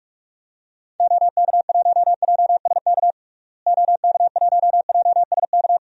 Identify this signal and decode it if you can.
Morse code, heard as a keyed tone, about 46 wpm, 700 Hz. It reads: OK1JSK OK1JSK
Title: DXCC-100 Call Signs - 6 Characters - Repeated 2x - 45wpm